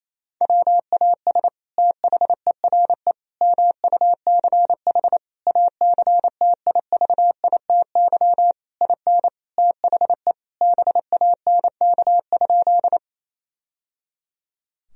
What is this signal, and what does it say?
Morse code, heard as a keyed tone, about 27 wpm, 700 Hz. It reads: WAS THERE MUCH ACTIVITY IN THE BANK?
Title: Sentences from Top 1000 Words - Review Part 4 - 28wpm